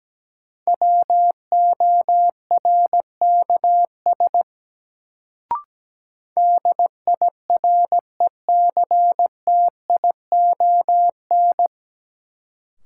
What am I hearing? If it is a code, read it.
Morse code, heard as a keyed tone, about 17 wpm, 700 Hz. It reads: WORKS DIRECTION
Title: Sentences from Top 1000 Words - Review Part 5 - New Word Drill - 17wpm